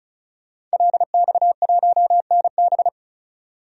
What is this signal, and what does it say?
Morse code, heard as a keyed tone, about 34 wpm, 700 Hz. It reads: LX1NB
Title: DXCC-100 Call Signs - 5 Characters - 35wpm